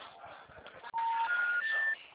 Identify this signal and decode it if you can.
Special information tone, with noise.
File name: out- -2015